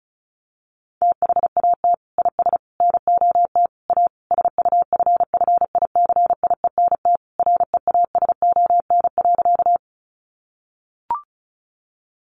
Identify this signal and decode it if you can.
Morse code, heard as a keyed tone, about 34 wpm, 700 Hz. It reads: THAT IS NOT A SUFFICIENT REASON.
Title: Sentences from Top 1400 Words 35wpm